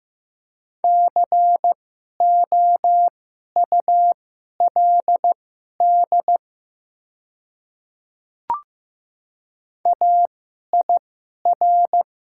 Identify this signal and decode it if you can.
Morse code, heard as a keyed tone, about 15 wpm, spaced as for 11 wpm, 700 Hz. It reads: COULD AIR
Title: 2x Character Spacing - Top 300 Words - 15wpm